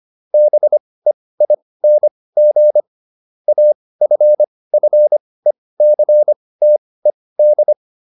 Morse code, heard as a keyed tone, about 25 wpm, spaced as for 18 wpm, 600 Hz. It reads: BEING AFFECTED